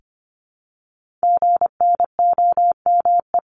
Morse code, heard as a keyed tone, about 25 wpm, 700 Hz. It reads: GNOME
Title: Multi-letter Phonemes Repeated 3x 25wpm